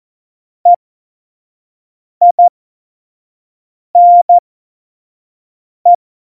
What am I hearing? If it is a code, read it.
Morse code, heard as a keyed tone, about 14 wpm, spaced as for 5 wpm, 700 Hz. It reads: EINE
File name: Morsekurs / Witze / Witz 14. November 2017, 25 F70